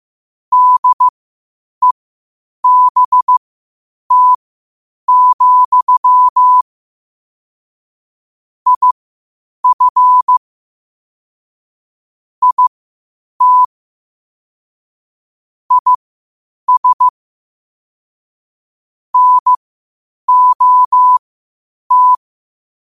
Morse code, heard as a keyed tone, about 15 wpm, spaced as for 8 wpm, 1000 Hz. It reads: DEBT, IF IT IS NOT
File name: qotdfiles / 8wpm